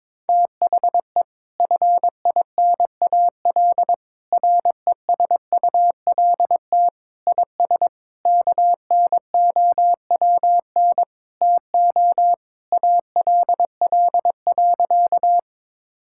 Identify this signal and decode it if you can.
Morse code, heard as a keyed tone, about 21 wpm, 700 Hz. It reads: THE FINAL RESULT IS KNOWN TO ALL.
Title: Sentences from Top 1000 Words - Review Part 5 - 22wpm